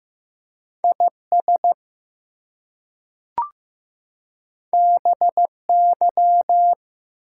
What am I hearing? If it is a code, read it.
Morse code, heard as a keyed tone, about 15 wpm, 700 Hz. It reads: IS BY
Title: Two Letter Words - 15wpm